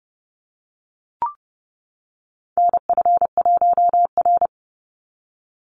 Morse code, heard as a keyed tone, about 30 wpm, 700 Hz. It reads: NF1R
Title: NA SOTA Chaser Callsigns 30wpm